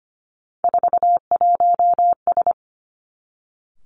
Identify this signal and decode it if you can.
Morse code, heard as a keyed tone, about 25 wpm, 700 Hz. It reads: 41S
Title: Three Challenging Letter-Number Combinations 25wpm